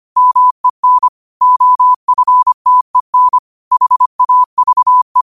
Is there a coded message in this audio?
Yes: Morse code from 0.2 to 5.2 s, heard as a keyed tone, about 25 wpm, 1000 Hz. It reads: MEN OFTEN HAVE